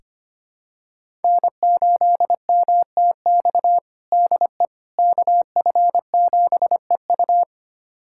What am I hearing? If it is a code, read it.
Morse code, heard as a keyed tone, about 25 wpm, 700 Hz. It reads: N8MTX DE KF7EU
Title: USA Callsign DE Callsign - 5 Characters - 25wpm